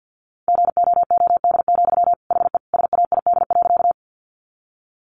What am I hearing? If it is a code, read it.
Morse code, heard as a keyed tone, about 50 wpm, 700 Hz. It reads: GOOD, HE SAID.
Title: Sentences from Top 200 Words 50wpm